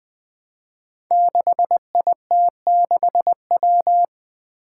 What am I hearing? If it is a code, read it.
Morse code, heard as a keyed tone, about 20 wpm, 700 Hz. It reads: 6IT6W
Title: Five Challenging Letter-Number Combinations 20wpm